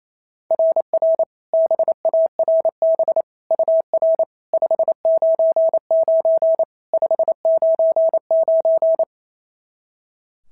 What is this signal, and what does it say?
Morse code, heard as a keyed tone, about 28 wpm, 690 Hz. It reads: RR BARB UR 599 599
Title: SOTA QSO Practice 28wpm